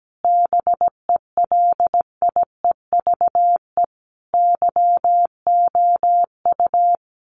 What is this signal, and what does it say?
Morse code, heard as a keyed tone, about 17 wpm, 700 Hz. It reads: BELIEVE YOU
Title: Sets of 5 Words Encore 17wpm